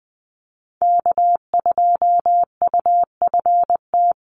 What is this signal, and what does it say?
Morse code, heard as a keyed tone, about 20 wpm, 700 Hz. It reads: K2UFT
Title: Call Signs Repeated 3x - 20wpm